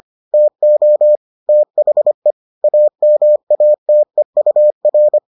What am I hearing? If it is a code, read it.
Morse code, heard as a keyed tone, about 25 wpm, 600 Hz. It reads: TO THE AMATEUR